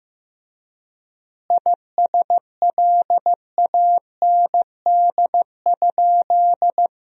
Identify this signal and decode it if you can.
Morse code, heard as a keyed tone, about 15 wpm, 700 Hz. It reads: ISLAND?
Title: 3x Word-Spacing - Sentences from Top 500 Words - 15wpm